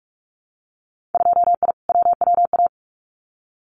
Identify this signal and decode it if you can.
Morse code, heard as a keyed tone, about 44 wpm, 700 Hz. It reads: 2I WWA
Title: Winter Field Day Exchanges Encore 45wpm